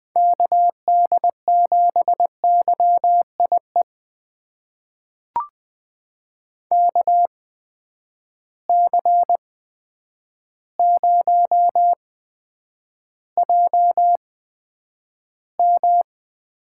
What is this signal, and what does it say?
Morse code, heard as a keyed tone, about 20 wpm, spaced as for 9 wpm, 700 Hz. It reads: KD7YIE KC0JM
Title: USA Call Signs - 6 Characters - 8x Character Spacing - 20wpm